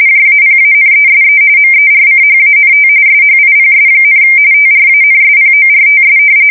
RTTY, about 45.45 baud. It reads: JAPANESE MERCHANT SHIPS, TRANSPORTS
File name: fsk170 / usnavy4